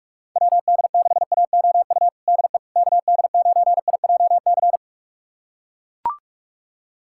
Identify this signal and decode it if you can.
Morse code, heard as a keyed tone, about 42 wpm, 690 Hz. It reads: WD6AOU DE KD9IJC
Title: USA Callsign DE Callsign - 6 Characters - 45wpm